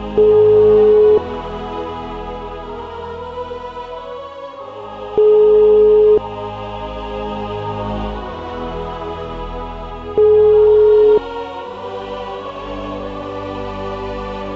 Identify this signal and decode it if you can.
Ringback tone (Europe, 425 Hz).